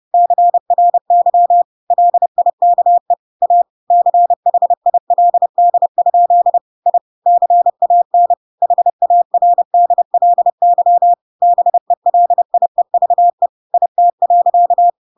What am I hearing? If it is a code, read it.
Morse code, heard as a keyed tone, about 30 wpm, 700 Hz. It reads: CRY LIKE A CHILD? I CAN HARDLY BELIEVE IT.
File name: Pinocchio / Pinocchio ch1 30 30 ik1hge